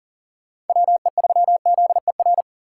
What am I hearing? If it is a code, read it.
Morse code, heard as a keyed tone, about 52 wpm, 700 Hz. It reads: WE3ZER